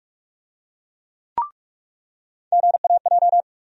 Morse code, heard as a keyed tone, about 46 wpm, 700 Hz. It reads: GAJ
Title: USA Call Sign Postfixes - 3 Letters 45wpm